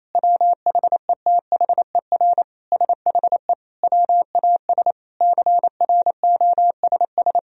Morse code, heard as a keyed tone, about 28 wpm, 700 Hz. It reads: WHETHER SHE WAS CROSS
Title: Sentences from Top 1000 Words - Review Part 3 - 28wpm